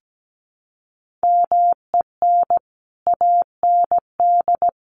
Morse code, heard as a keyed tone, about 17 wpm, 700 Hz. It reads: MEN AND
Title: Sets of 3 Words 17wpm